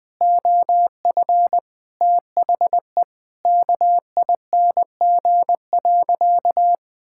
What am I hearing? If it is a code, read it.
Morse code, heard as a keyed tone, about 20 wpm, 700 Hz. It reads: OF THE KING.
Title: Sentences from Top 1000 Words - Review Part 6 - 20wpm